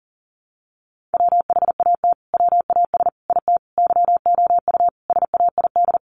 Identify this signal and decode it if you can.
Morse code, heard as a keyed tone, about 40 wpm, 700 Hz. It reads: WHAT WAS IT YOU SAID
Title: Sentences from Top 200 Words 40wpm